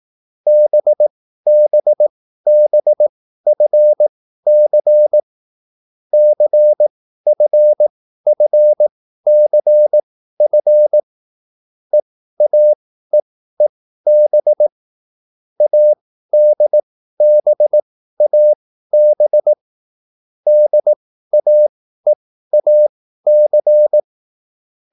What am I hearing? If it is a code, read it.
Morse code, heard as a keyed tone, about 18 wpm, spaced as for 13 wpm, 600 Hz. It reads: BBBFC CFFCF EAEEB ADBAB DAEAC